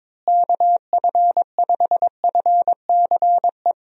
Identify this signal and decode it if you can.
Morse code, heard as a keyed tone, about 22 wpm, 700 Hz. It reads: KF5FCE
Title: USA Call Signs - 6 Characters - Repeated 2x - 22wpm